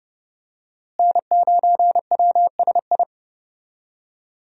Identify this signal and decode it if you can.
Morse code, heard as a keyed tone, about 29 wpm, 700 Hz. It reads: N9WSI
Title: USA Call Signs - 5 Characters - Repeated 3x - 30wpm